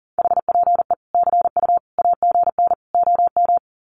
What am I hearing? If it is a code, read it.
Morse code, heard as a keyed tone, about 38 wpm, 700 Hz. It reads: HPE CU AGN OM
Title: Common QSO Phrases 40wpm